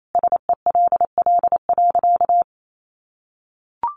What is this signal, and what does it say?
Morse code, heard as a keyed tone, about 28 wpm, 700 Hz. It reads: SELL.
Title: Sentences from Top 1100 Words 28wpm